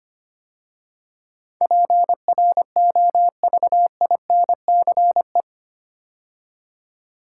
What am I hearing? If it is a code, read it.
Morse code, heard as a keyed tone, about 25 wpm, 700 Hz. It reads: PROVINCE
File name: 8 Letter Words - First 3 Letters Given - 25wpm